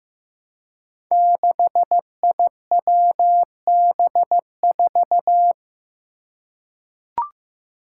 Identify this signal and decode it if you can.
Morse code, heard as a keyed tone, about 15 wpm, 700 Hz. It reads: 6IWB4
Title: Five Challenging Letter-Number Combinations 15wpm